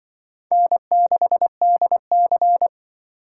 Morse code, heard as a keyed tone, about 25 wpm, 700 Hz. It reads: N6DC
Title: Speed Racing - USA 1x2 Call Signs 20wpm